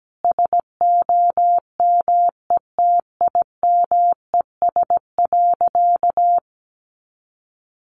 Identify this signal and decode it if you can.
Morse code, heard as a keyed tone, about 17 wpm, 700 Hz. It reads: SOMETIMES.
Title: Sentences from Top 600 Words 17wpm